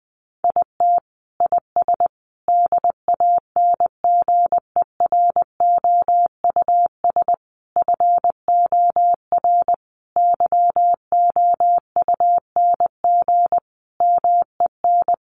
Morse code, heard as a keyed tone, about 20 wpm, 700 Hz. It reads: IT IS DANGEROUS FOR YOUNG MEN
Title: Sentences from Top 1200 Words 20wpm